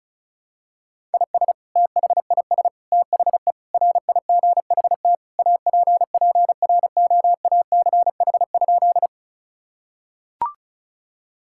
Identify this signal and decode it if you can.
Morse code, heard as a keyed tone, about 36 wpm, 700 Hz. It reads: IS THIS THE RIGHT APPROACH?
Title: Sentences from Top 1000 Words - Review Part 6 - 35wpm